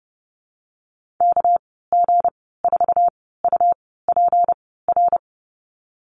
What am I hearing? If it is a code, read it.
Morse code, heard as a keyed tone, about 30 wpm, spaced as for 17 wpm, 700 Hz. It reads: KG4UPR
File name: Lesson 37g - Letter x - Call Signs 30wpm